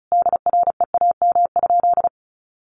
Morse code, heard as a keyed tone, about 34 wpm, 700 Hz. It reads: DREAM?
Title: Sentences from Top 1500 Words 35wpm